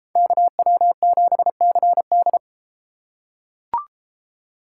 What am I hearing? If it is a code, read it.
Morse code, heard as a keyed tone, about 32 wpm, 700 Hz. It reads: KW7CD